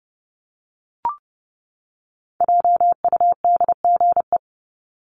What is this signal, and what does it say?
Morse code, heard as a keyed tone, about 28 wpm, 700 Hz. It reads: JUDGE
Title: Top 1751-2000 Words 30wpm